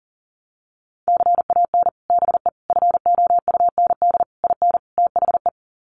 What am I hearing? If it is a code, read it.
Morse code, heard as a keyed tone, about 40 wpm, 700 Hz. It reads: CAN BE FOUND IN THE
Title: Sets of 5 Words Encore 40wpm